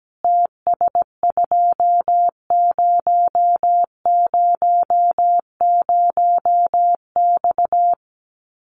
Morse code, heard as a keyed tone, about 17 wpm, 700 Hz. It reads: TS2000X
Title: Common QSO Phrases 17wpm